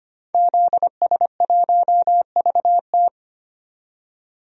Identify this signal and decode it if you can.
Morse code, heard as a keyed tone, about 25 wpm, 700 Hz. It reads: ZS1VT
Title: DXCC-100 Call Signs - 5 Characters - 25wpm